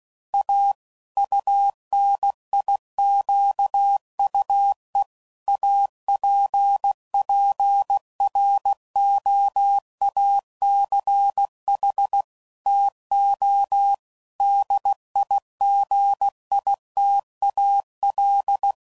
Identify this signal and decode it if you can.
Morse code, heard as a keyed tone, about 16 wpm, 800 Hz. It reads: A UNIQUE APPROACH TO DIGITAL
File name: CW Code Groups @15WPM / Technical Subjects / Gates Intro